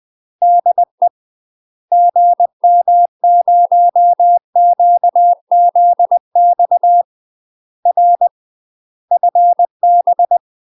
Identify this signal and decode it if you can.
Morse code, heard as a keyed tone, about 20 wpm, 700 Hz. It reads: DE GM0QZX R FB
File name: UK MASTER QSOs TRAINING/20WPM